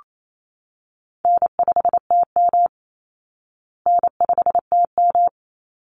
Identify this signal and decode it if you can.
Morse code, heard as a keyed tone, about 28 wpm, 700 Hz. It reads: N5TM N5TM
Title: Call Signs Repeated 3x - 28wpm